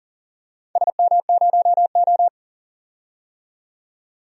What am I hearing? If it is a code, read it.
Morse code, heard as a keyed tone, about 40 wpm, 700 Hz. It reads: SM0O
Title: DXCC-100 Call Signs - 4 Characters - 40wpm